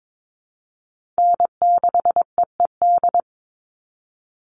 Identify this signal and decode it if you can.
Morse code, heard as a keyed tone, about 22 wpm, 700 Hz. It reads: N6EED
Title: Call Signs - 22wpm